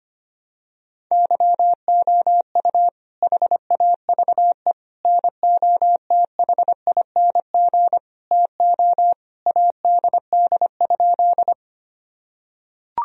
Morse code, heard as a keyed tone, about 25 wpm, 700 Hz. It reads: YOU HAVE NOTHING TO ADD?
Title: Sentences from Top 1000 Words - Review Part 1 - 25wpm